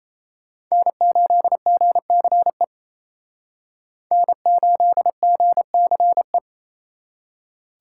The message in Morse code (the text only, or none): N8GCE N8GCE